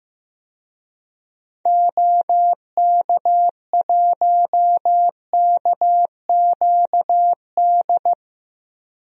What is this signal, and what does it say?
Morse code, heard as a keyed tone, about 15 wpm, 700 Hz. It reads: OK1KQD
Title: DXCC-100 Call Signs - 6 Characters - 15wpm